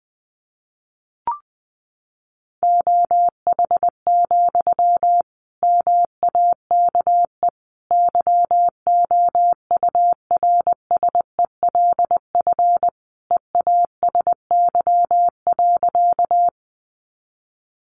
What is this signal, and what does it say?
Morse code, heard as a keyed tone, about 20 wpm, 700 Hz. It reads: OH, MAKE YOURSELF EASY.
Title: Sentences from Top 700 Words 20wpm